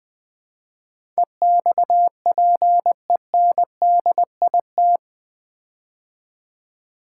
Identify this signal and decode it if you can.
Morse code, heard as a keyed tone, about 20 wpm, 700 Hz. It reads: EXPENDIT
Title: Build A Word - 4 Letters At A Time - 11-Letter Words - 20wpm